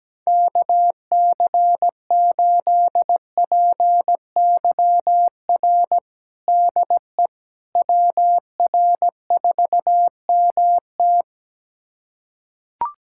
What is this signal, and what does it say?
Morse code, heard as a keyed tone, about 17 wpm, 700 Hz. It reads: KC8PYR DE WR4MT
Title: USA Callsign DE Callsign 17wpm